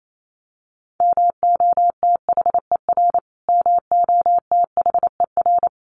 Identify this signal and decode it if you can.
Morse code, heard as a keyed tone, about 28 wpm, 700 Hz. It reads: MOTHER MOTHER
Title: Top 200 Words Repeated 28wpm